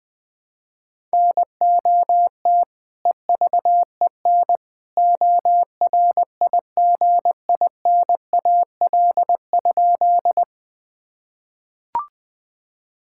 Morse code, heard as a keyed tone, about 20 wpm, 700 Hz. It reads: NOT EVEN ORIGINAL?
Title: Sentences from Top 1000 Words - Review Part 6 - 20wpm